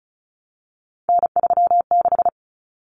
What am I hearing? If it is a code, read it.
Morse code, heard as a keyed tone, about 34 wpm, 700 Hz. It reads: N36
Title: Common Cut Numbers - 3-Digit 35wpm